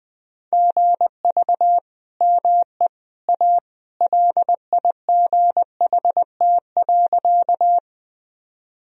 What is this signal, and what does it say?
Morse code, heard as a keyed tone, about 20 wpm, 700 Hz. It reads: GV ME A LIGHT.
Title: Sentences from Top 200 Words and Abbreviations 20wpm